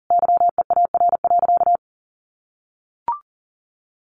Morse code, heard as a keyed tone, about 38 wpm, 700 Hz. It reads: YEAR.
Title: Sentences from Top 1200 Words 40wpm